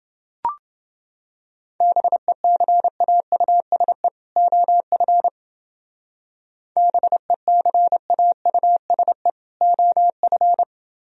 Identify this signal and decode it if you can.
Morse code, heard as a keyed tone, about 29 wpm, 700 Hz. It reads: BECAUSE OF BECAUSE OF